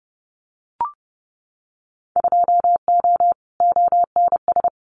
Morse code, heard as a keyed tone, about 30 wpm, 700 Hz. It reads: 2O ONS
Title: Winter Field Day Exchanges Encore 30wpm